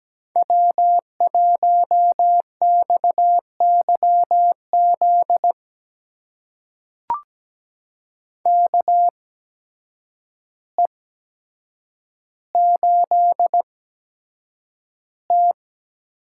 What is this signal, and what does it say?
Morse code, heard as a keyed tone, about 17 wpm, spaced as for 8 wpm, 700 Hz. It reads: W1XYZ KE8T
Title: USA Call Signs - 5 Characters - 8x Character Spacing - 17wpm